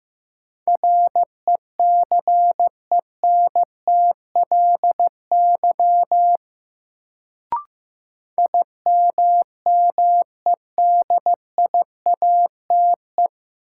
Morse code, heard as a keyed tone, about 15 wpm, 700 Hz. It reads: RECENTLY IMMEDIATE
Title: Sentences from Top 1200 Words - New Word Drill - 15wpm